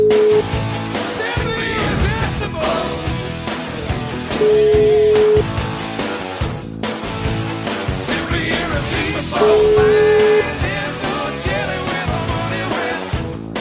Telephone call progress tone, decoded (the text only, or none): ringback tone